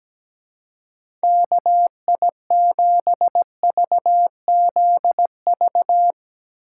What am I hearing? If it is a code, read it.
Morse code, heard as a keyed tone, about 17 wpm, 700 Hz. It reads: KI7VZV
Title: USA Call Signs - 2x Character Spacing - 17wpm